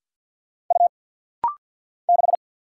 Morse code, heard as a keyed tone, about 49 wpm, 700 Hz. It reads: U 6
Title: Single Letter-Number Alt - Rapid-Fire 50wpm